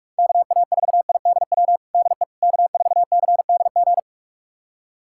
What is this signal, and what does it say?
Morse code, heard as a keyed tone, about 41 wpm, 700 Hz. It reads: KA4IDW DE K4CDG
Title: USA Callsign DE Callsign 45wpm